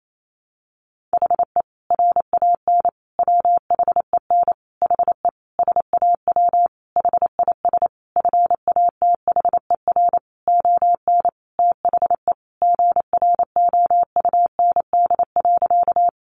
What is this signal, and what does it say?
Morse code, heard as a keyed tone, about 28 wpm, 700 Hz. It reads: HE RAN WHEN HE SAW HIS FATHER ON THE GROUND.
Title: Sentences from Top 500 Words 28wpm